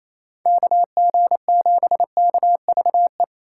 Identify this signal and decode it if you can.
Morse code, heard as a keyed tone, about 37 wpm, 700 Hz. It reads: KG7KVE